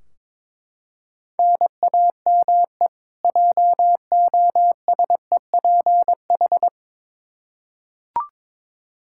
Morse code, heard as a keyed tone, about 22 wpm, 700 Hz. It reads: NAME JOSEPH
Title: Practice QSO - Exchange 1 - Beginning - 22wpm